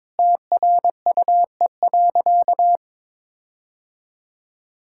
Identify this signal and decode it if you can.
Morse code, heard as a keyed tone, about 22 wpm, 700 Hz. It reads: TRUE.
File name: Sentences with 7-Word Phrases - 22wpm